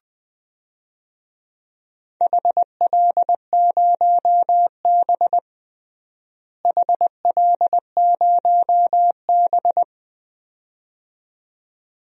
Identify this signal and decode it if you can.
Morse code, heard as a keyed tone, about 21 wpm, 700 Hz. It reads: HL0B HL0B
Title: DXCC-100 Call Signs - 4 Characters - Repeated 2x - 20wpm